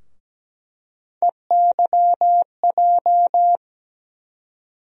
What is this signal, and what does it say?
Morse code, heard as a keyed tone, about 17 wpm, 700 Hz. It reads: EYJ
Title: USA Call Sign Postfixes - 3 Letters 17wpm